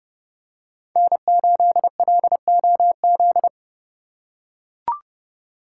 Morse code, heard as a keyed tone, about 28 wpm, 700 Hz. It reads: N8LOZ